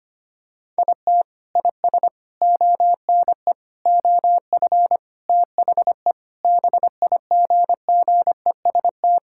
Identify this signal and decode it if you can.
Morse code, heard as a keyed tone, about 25 wpm, 700 Hz. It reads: IT IS ONE OF THE BIGGEST
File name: practice / 25wpm / Sentences with 4-Word Phrases - 25wpm